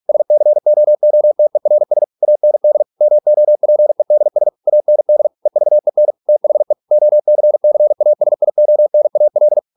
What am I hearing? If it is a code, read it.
Morse code, heard as a keyed tone, about 46 wpm, 600 Hz. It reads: SCOOTERS AND MOPEDS AND EVEN THE OCCASIONAL